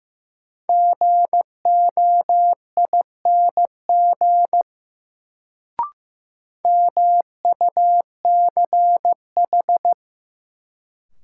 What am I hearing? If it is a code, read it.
Morse code, heard as a keyed tone, about 15 wpm, 700 Hz. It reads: GOING MUCH
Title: Sets of 2 Words - New Word Drill - 15wpm